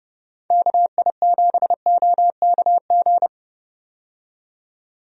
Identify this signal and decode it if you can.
Morse code, heard as a keyed tone, about 30 wpm, 700 Hz. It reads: KI7OKG
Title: USA Call Signs - 6 Characters - 30wpm